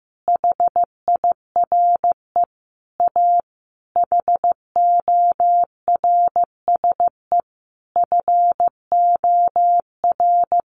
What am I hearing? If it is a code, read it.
Morse code, heard as a keyed tone, about 15 wpm, 700 Hz. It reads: HIRE A HORSE FOR
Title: Sentences from Top 1400 Words 15wpm